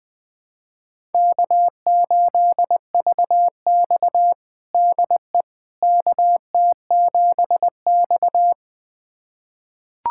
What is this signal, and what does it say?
Morse code, heard as a keyed tone, about 21 wpm, 700 Hz. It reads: K8VX DE KT7X
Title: USA Callsign DE Callsign - 4 Characters - 20wpm